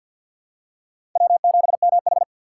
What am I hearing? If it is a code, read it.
Morse code, heard as a keyed tone, about 49 wpm, 700 Hz. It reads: W7MH
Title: USA Call Signs - 4 Characters - 2x Character Spacing - 50wpm